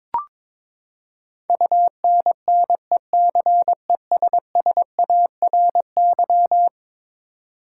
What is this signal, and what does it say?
Morse code, heard as a keyed tone, about 22 wpm, 700 Hz. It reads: UNNECESSARY
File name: practice / 22wpm / Eleven Letter Words - 22wpm